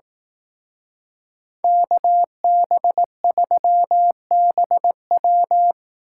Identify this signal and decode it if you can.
Morse code, heard as a keyed tone, about 18 wpm, 700 Hz. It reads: KB3BW
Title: Speed Racing - USA Call Signs 15wpm